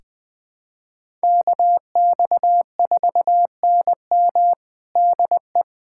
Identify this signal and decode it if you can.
Morse code, heard as a keyed tone, about 20 wpm, 690 Hz. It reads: KX4NM DE
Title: USA Callsign DE Callsign 20wpm